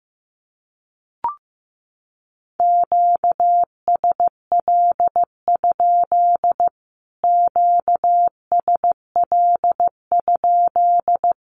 Morse code, heard as a keyed tone, about 14 wpm, 700 Hz. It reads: QSL? QSL?
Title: Q-Codes Repeated 3x 15wpm